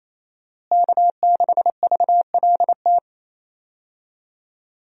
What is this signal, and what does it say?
Morse code, heard as a keyed tone, about 28 wpm, 700 Hz. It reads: K6VLT
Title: USA Call Signs - 5 Characters - Repeated 3x - 28wpm